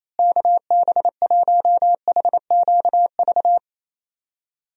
Morse code, heard as a keyed tone, about 27 wpm, 700 Hz. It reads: KB1HQV
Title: USA Call Signs - 6 Characters - Repeated 2x - 28wpm